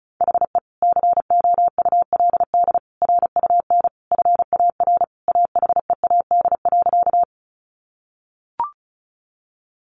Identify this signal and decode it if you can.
Morse code, heard as a keyed tone, about 34 wpm, 700 Hz. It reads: HE COULD RUN FAR AHEAD.
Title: Sentences from Top 1100 Words 35wpm